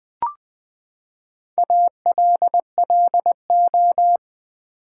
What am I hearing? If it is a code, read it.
Morse code, heard as a keyed tone, about 20 wpm, 700 Hz. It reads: ALLO
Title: Build A Word - 4 Letters At A Time - 10-Letter Words - 20wpm